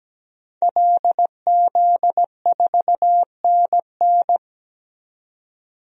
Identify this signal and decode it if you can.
Morse code, heard as a keyed tone, about 17 wpm, 700 Hz. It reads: LZ4NN
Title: DXCC-100 Call Signs - 5 Characters - 17wpm